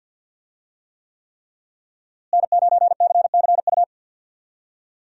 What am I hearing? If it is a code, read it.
Morse code, heard as a keyed tone, about 50 wpm, 700 Hz. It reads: N9CCU
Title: USA Call Signs - 5 Characters - Repeated 3x - 50wpm